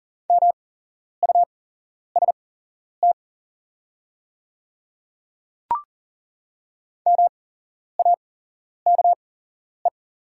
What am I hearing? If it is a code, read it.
Morse code, heard as a keyed tone, about 42 wpm, spaced as for 11 wpm, 700 Hz. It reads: MUST MAKE
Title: 8x Character Spacing - Top 200 Words - 40wpm